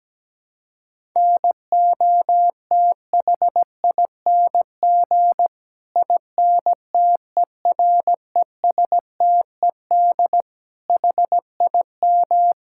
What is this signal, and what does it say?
Morse code, heard as a keyed tone, about 17 wpm, 700 Hz. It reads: NOTHING INTERESTED HIM
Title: Sentences from Top 1000 Words - Review Part 6 - 17wpm